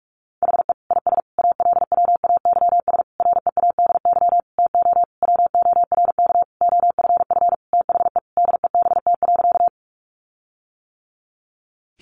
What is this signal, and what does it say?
Morse code, heard as a keyed tone, about 44 wpm, 700 Hz. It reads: HE IS ALWAYS READY TO WORK OFF THE DEBT.
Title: Sentences from Top 1300 Words 45wpm